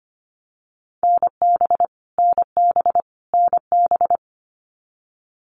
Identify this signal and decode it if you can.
Morse code, heard as a keyed tone, about 25 wpm, 700 Hz. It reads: NB NB NB
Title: Canadian Provinces and Territories - Repeated 3x - 25wpm